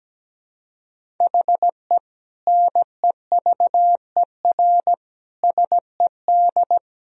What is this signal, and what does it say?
Morse code, heard as a keyed tone, about 17 wpm, 700 Hz. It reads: HE NEVER SED
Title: Sentences from Top 500 Words and Abbreviations 17wpm